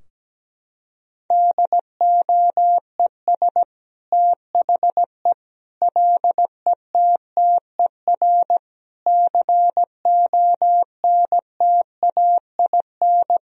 Morse code, heard as a keyed tone, about 17 wpm, 700 Hz. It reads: DOES THE LETTER CONTAIN
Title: Sentences from Top 1000 Words - Review Part 2 - 17wpm